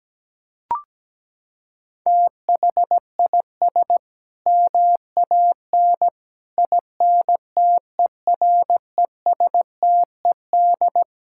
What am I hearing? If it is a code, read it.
Morse code, heard as a keyed tone, about 17 wpm, 700 Hz. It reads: THIS MAN INTERESTED
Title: Sentences from Top 1000 Words - Review Part 6 - 17wpm